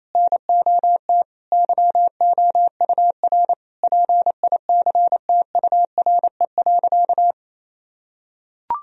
Morse code, heard as a keyed tone, about 28 wpm, 700 Hz. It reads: NOT YOUR PICTURE.
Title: Sentences from Top 1000 Words - Review Part 1 - 28wpm